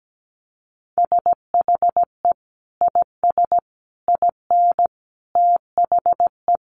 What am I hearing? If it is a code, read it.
Morse code, heard as a keyed tone, about 17 wpm, 700 Hz. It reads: SHE IS IN THE
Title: Sentences from Top 600 Words 17wpm